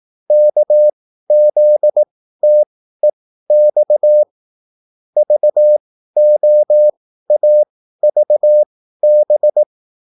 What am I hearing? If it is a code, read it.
Morse code, heard as a keyed tone, about 18 wpm, spaced as for 13 wpm, 600 Hz. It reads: KZTEX VOAVB